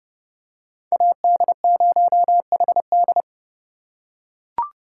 Morse code, heard as a keyed tone, about 30 wpm, 700 Hz. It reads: AD0HD
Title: USA Call Signs - 5 Characters - 3x Character Spacing - 30wpm